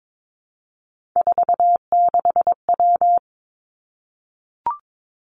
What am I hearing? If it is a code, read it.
Morse code, heard as a keyed tone, about 22 wpm, 700 Hz. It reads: 46W
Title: Three Challenging Letter-Number Combinations 22wpm